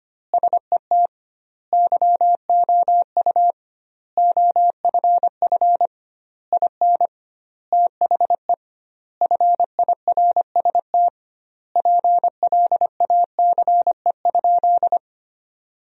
Morse code, heard as a keyed tone, about 25 wpm, 700 Hz. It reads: SET YOU OFF IN THE FIRST PLACE?
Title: 2x Word-Spacing - Sentences from Top 200 Words - 25wpm